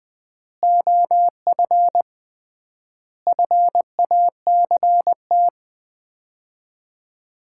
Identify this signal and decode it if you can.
Morse code, heard as a keyed tone, about 20 wpm, 700 Hz. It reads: OF FACT
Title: 3x Word-Spacing - Sets of 4 Words - 20wpm